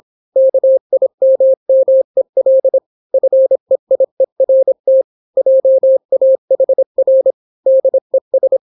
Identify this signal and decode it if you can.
Morse code, heard as a keyed tone, about 26 wpm, 530 Hz. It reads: KIMMEL FEIERT JAHR DES